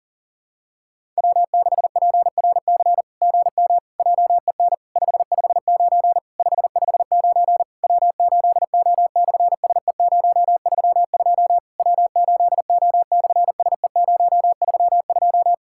Morse code, heard as a keyed tone, about 37 wpm, 700 Hz. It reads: W6PRC GM JEN 559 559 W8O/SE032 W8O/SE032
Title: SOTA QSO Practice 40wpm